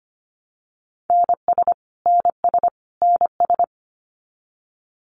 Morse code, heard as a keyed tone, about 25 wpm, 700 Hz. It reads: NS NS NS
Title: Canadian Provinces and Territories - Repeated 3x - 25wpm